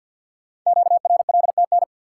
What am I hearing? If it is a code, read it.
Morse code, heard as a keyed tone, about 48 wpm, 700 Hz. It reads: QRLTN
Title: Code Groups - 5 Letters - 50wpm